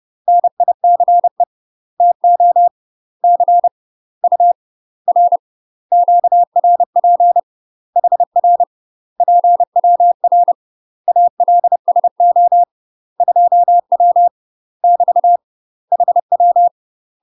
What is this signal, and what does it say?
Morse code, heard as a keyed tone, about 30 wpm, 700 Hz. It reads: NICE TO C U R QRP HR PWR ALSO 2W <BT> HW
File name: UK MASTER QSOs TRAINING / 30WPM / UK MASTER QSOs 0039